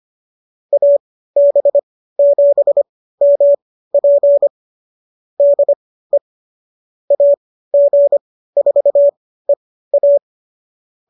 Morse code, heard as a keyed tone, about 25 wpm, spaced as for 15 wpm, 570 Hz. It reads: AB7MP DE AG4EA